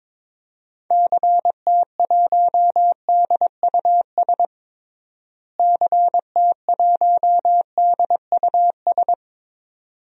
Morse code, heard as a keyed tone, about 22 wpm, 700 Hz. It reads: CT1DUS CT1DUS
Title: DXCC-100 Call Signs - 6 Characters - Repeated 2x - 22wpm